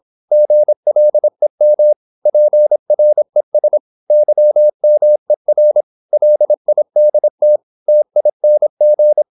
Morse code, heard as a keyed tone, about 26 wpm, 600 Hz. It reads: GLEM PRES YMER LIDT TING
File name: Korte ord DK 26wpm